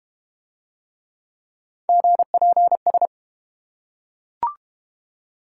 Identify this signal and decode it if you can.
Morse code, heard as a keyed tone, about 31 wpm, 700 Hz. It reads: GPS
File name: Easy Words - Speed-Racing with Hints - 25wpm